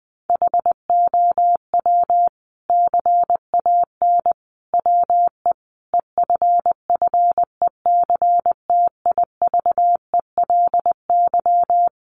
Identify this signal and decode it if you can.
Morse code, heard as a keyed tone, about 19 wpm, 700 Hz. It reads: HOW CAN WE EFFECTIVELY
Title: Sentences from Top 1400 Words 20wpm